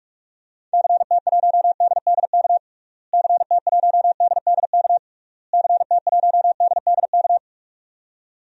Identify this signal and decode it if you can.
Morse code, heard as a keyed tone, about 46 wpm, 700 Hz. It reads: CT1DDK CT1DDK CT1DDK
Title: DXCC-100 Call Signs - 6 Characters - Repeated 3x - 45wpm